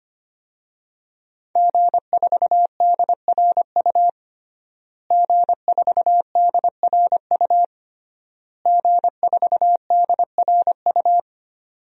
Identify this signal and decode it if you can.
Morse code, heard as a keyed tone, about 25 wpm, 700 Hz. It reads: G4DRU G4DRU G4DRU
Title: DXCC-100 Call Signs - 5 Characters - Repeated 3x - 25wpm